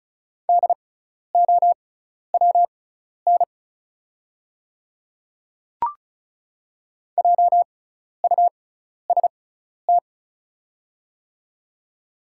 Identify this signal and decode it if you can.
Morse code, heard as a keyed tone, about 36 wpm, spaced as for 12 wpm, 690 Hz. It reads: DOWN JUST
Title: 6x Character Spacing - Top 200 Words - 35wpm